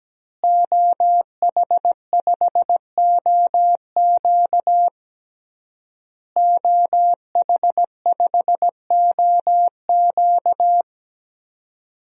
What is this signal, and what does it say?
Morse code, heard as a keyed tone, about 17 wpm, 700 Hz. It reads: OH5OQ OH5OQ
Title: DXCC-100 Call Signs - 5 Characters - Repeated 3x - 17wpm